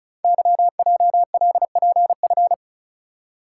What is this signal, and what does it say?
Morse code, heard as a keyed tone, about 36 wpm, 700 Hz. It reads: YJLPF
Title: Code Groups - 5 Letters - 35wpm